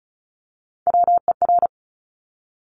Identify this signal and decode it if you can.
Morse code, heard as a keyed tone, about 36 wpm, 700 Hz. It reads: WER
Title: Top Three Letter Combinations in Words 35wpm